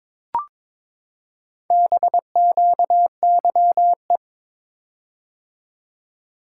Morse code, heard as a keyed tone, about 22 wpm, 700 Hz. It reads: BQYE
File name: practice / 22wpm / Four Letter-Number Combinations - 22wpm